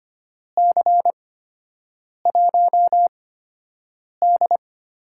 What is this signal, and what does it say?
Morse code, heard as a keyed tone, about 25 wpm, spaced as for 7 wpm, 700 Hz. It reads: C1D
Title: USA Call Signs - 8x Character Spacing - 25wpm